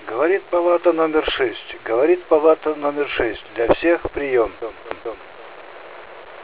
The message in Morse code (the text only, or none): none